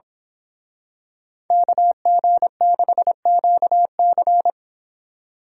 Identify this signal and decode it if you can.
Morse code, heard as a keyed tone, about 26 wpm, 700 Hz. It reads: KG6QC
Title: Speed Racing - USA 2x2 Call Signs 22wpm